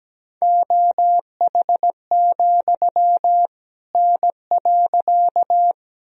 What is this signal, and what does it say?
Morse code, heard as a keyed tone, about 17 wpm, 700 Hz. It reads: OH, N.
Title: Sentences from Top 500 Words and Abbreviations 17wpm